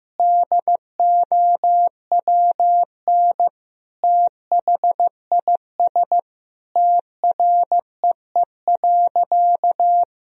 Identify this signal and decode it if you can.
Morse code, heard as a keyed tone, about 16 wpm, 700 Hz. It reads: DOWN THIS TREE.
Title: Sentences from Top 1000 Words - Review Part 1 - 15wpm